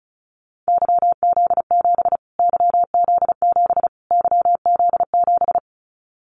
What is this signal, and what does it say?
Morse code, heard as a keyed tone, about 34 wpm, 700 Hz. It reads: YZ7 YZ7 YZ7
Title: Callsign Prefixes for Top 25 DXCC - Repeated 3x - 35wpm